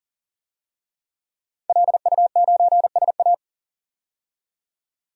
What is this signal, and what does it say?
Morse code, heard as a keyed tone, about 40 wpm, 700 Hz. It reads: LU9SA
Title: DXCC-100 Call Signs - 5 Characters - 40wpm